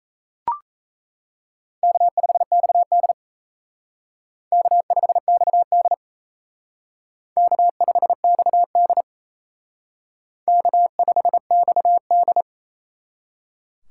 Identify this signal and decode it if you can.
Morse code, heard as a keyed tone, about 33 wpm, 700 Hz. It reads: K5XD K5XD K5XD K5XD
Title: Speed Racing - USA Call Signs 28wpm